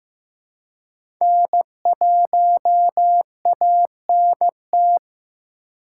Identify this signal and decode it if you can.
Morse code, heard as a keyed tone, about 16 wpm, 700 Hz. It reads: N1ANT
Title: USA Call Signs - 5 Characters - 2x Character Spacing - 15wpm